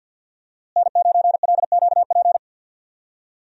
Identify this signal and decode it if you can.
Morse code, heard as a keyed tone, about 46 wpm, 690 Hz. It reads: N9LQP
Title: USA Call Signs - 5 Characters - Repeated 3x - 50wpm